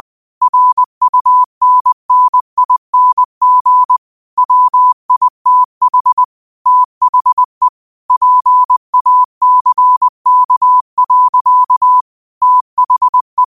Morse code, heard as a keyed tone, about 20 wpm, 1000 Hz. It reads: RUNNING WITH THE PACK. THE
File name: qotdfiles / 20wpm